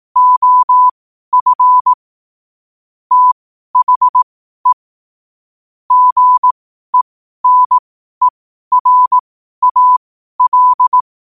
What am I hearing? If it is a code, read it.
Morse code, heard as a keyed tone, about 18 wpm, spaced as for 12 wpm, 1000 Hz. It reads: OF THE GENERAL
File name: qotdfiles / 12wpm